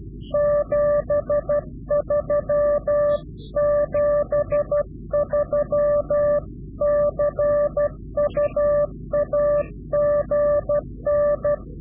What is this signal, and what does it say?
Morse code, heard as a keyed tone, about 12 wpm, 590 Hz. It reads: 73 73 CU AGN
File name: CW-QSO / Directional Call / Directional QSO